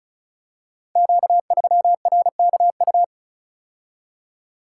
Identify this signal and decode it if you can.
Morse code, heard as a keyed tone, about 36 wpm, 690 Hz. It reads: Q3RKU
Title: Code Groups - 5 Letter:Numbers - 35wpm